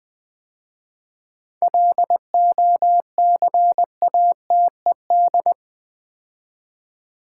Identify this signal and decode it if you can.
Morse code, heard as a keyed tone, about 20 wpm, 700 Hz. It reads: LOCATED
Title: 7 Letter Words - First 3 Letters Given - 20wpm